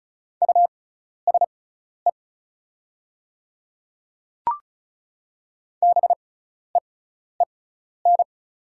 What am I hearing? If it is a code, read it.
Morse code, heard as a keyed tone, about 34 wpm, spaced as for 12 wpm, 700 Hz. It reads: USE BEEN
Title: 6x Character Spacing - Top 200 Words - 35wpm